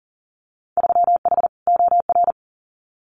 Morse code, heard as a keyed tone, about 41 wpm, 700 Hz. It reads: 3H OR
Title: Winter Field Day Exchanges Encore 40wpm